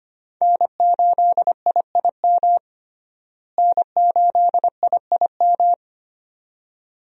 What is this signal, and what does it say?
Morse code, heard as a keyed tone, about 25 wpm, 700 Hz. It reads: N8IIM N8IIM
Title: USA Call Signs - 5 Characters - Repeated 2x - 25wpm